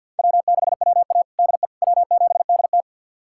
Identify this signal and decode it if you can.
Morse code, heard as a keyed tone, about 53 wpm, 700 Hz. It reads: W6WA DE W7DT
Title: USA Callsign DE Callsign - 4 Characters - 50wpm